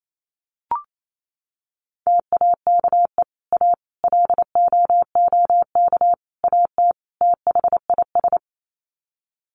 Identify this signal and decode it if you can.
Morse code, heard as a keyed tone, about 28 wpm, 700 Hz. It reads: TAKE A LOOK AT THIS
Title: Sets of 5 Words 28wpm